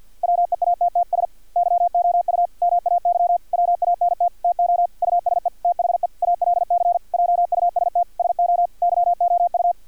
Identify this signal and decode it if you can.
Morse code, heard as a keyed tone, about 50 wpm, 700 Hz, with radio noise. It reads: WEATTS YOU MAY WANT TO USE THE ALC JUST SO YOU
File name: archived / CRSnet